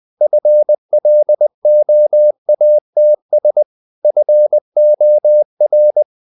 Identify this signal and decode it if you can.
Morse code, heard as a keyed tone, about 20 wpm, 600 Hz. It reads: FLOATS FOR
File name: the old man and the sea / the old man and the sea part 1 20wpm